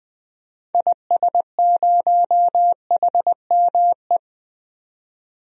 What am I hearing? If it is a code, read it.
Morse code, heard as a keyed tone, about 20 wpm, 700 Hz. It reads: IS0HME
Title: DXCC-100 Call Signs - 6 Characters - 20wpm